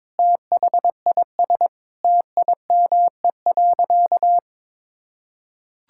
Morse code, heard as a keyed tone, about 22 wpm, 700 Hz. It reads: THIS TIME.
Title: Sentences from Top 1000 Words - Review Part 5 - 22wpm